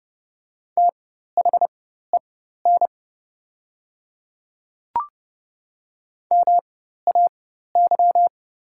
Morse code, heard as a keyed tone, about 30 wpm, spaced as for 14 wpm, 700 Hz. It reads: THEN MAY
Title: 4x Character Spacing - Top 300 Words - 30wpm